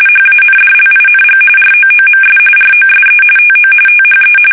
RTTY, about 45.45 baud. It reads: HANDLING ROOMS ROSE TO THE
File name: fsk850 / battleshipnj